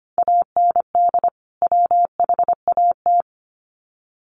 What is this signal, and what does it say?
Morse code, heard as a keyed tone, about 25 wpm, 700 Hz. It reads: AND WHAT
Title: Sets of 2 Words 25wpm